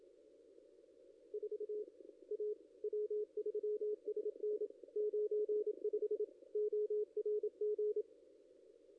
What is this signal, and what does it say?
Morse code, heard as a keyed tone, about 27 wpm, 420 Hz. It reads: 4 A W3SN 95 ORG
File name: contests / ss-cw-k5tr-op-n5rz -1600